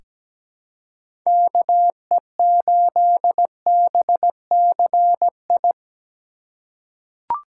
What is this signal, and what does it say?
Morse code, heard as a keyed tone, about 17 wpm, 700 Hz. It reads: KE8BCI
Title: USA Call Signs - 6 Characters - 2x Character Spacing - 17wpm